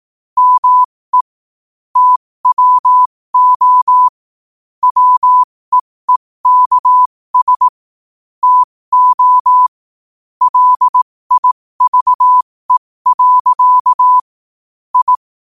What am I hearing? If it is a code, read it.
Morse code, heard as a keyed tone, about 18 wpm, spaced as for 15 wpm, 1000 Hz. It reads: ME TWO WEEKS TO LIVE. I
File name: qotdfiles / 15wpm